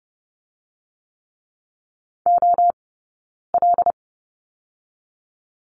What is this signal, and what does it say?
Morse code, heard as a keyed tone, about 30 wpm, 700 Hz. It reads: O L
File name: Lesson 17d - Letter u - Two Characters 30wpm